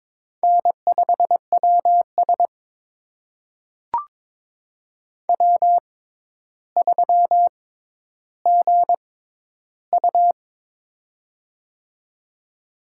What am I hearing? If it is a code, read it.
Morse code, heard as a keyed tone, about 22 wpm, spaced as for 11 wpm, 700 Hz. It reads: N5WS W3GU
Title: USA Call Signs - 4 Characters - 6x Character Spacing - 22wpm